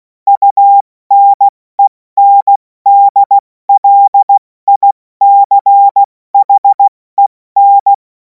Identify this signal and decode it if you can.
Morse code, heard as a keyed tone, about 16 wpm, 800 Hz. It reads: UNENDLICHEN
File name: Morsekurs / Witze / Witz 30. Juni 2026 70 F80